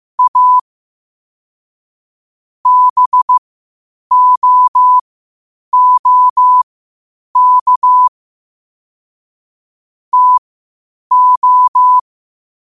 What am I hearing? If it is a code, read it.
Morse code, heard as a keyed tone, about 15 wpm, spaced as for 8 wpm, 1000 Hz. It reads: A BOOK TO
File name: qotdfiles / 8wpm